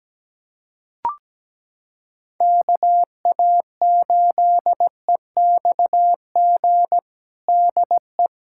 Morse code, heard as a keyed tone, about 17 wpm, 700 Hz. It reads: KA8EXG DE
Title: USA Callsign DE Callsign 17wpm